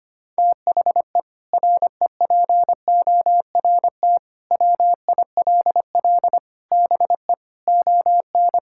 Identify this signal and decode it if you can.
Morse code, heard as a keyed tone, about 25 wpm, 700 Hz. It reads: THE REPORT WILL BE ON
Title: Sentences from Top 1000 Words - Review Part 6 - 25wpm